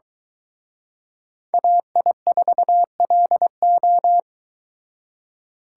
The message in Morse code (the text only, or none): AI4LO